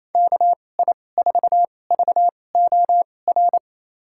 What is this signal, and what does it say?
Morse code, heard as a keyed tone, about 28 wpm, spaced as for 20 wpm, 700 Hz. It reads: KI4VOR
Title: USA Call Signs - 6 Characters - 2x Character Spacing - 28wpm